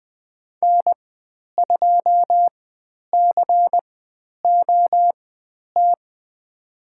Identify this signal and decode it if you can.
Morse code, heard as a keyed tone, about 20 wpm, spaced as for 10 wpm, 700 Hz. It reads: N2COT
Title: 102 Lesson 17g - Letter u - Call Signs 20wpm